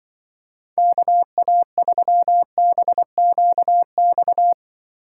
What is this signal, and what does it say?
Morse code, heard as a keyed tone, about 24 wpm, 700 Hz. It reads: KA3BQX
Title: Speed Racing - USA 2x3 Call Signs 20wpm